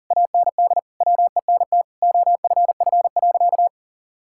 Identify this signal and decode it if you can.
Morse code, heard as a keyed tone, about 40 wpm, 700 Hz. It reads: AND WENT OFF.
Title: Sentences from Top 1600 Words 40wpm